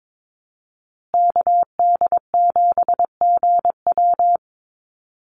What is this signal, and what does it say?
Morse code, heard as a keyed tone, about 22 wpm, 700 Hz. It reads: KD7GW
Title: USA Call Signs Repeated 2x 22wpm